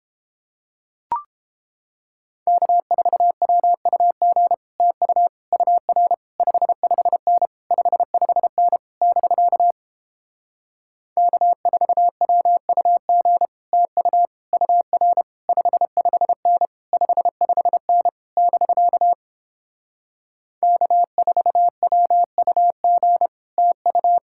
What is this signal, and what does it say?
Morse code, heard as a keyed tone, about 31 wpm, 700 Hz. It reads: K4WUG TU UR 55N 55N <BK> K4WUG TU UR 55N 55N <BK> K4WUG TU